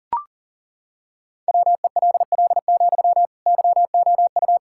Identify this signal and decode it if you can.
Morse code, heard as a keyed tone, about 40 wpm, 700 Hz. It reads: WELL, YOU
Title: Sentences from Top 1000 Words - Review Part 6 - 40wpm